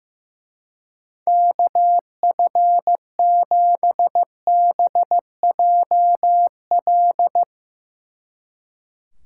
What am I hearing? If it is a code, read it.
Morse code, heard as a keyed tone, about 16 wpm, 700 Hz. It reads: KF7BJL
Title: Speed Racing - USA 2x3 Call Signs 15wpm